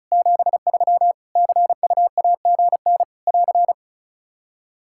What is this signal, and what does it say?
Morse code, heard as a keyed tone, about 34 wpm, 700 Hz. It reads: 73 CUAGN <AR>
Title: K1USN SST QSO Practice 35wpm